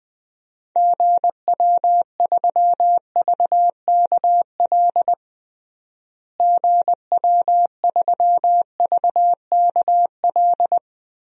Morse code, heard as a keyed tone, about 20 wpm, 700 Hz. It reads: GW3VKL GW3VKL
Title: DXCC-100 Call Signs - 6 Characters - Repeated 3x - 20wpm